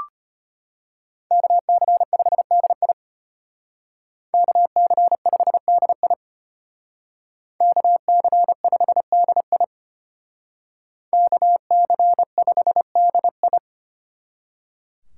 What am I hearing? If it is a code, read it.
Morse code, heard as a keyed tone, about 31 wpm, 700 Hz. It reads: KC5DI KC5DI KC5DI KC5DI
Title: Speed Racing - USA 2x2 Call Signs 25wpm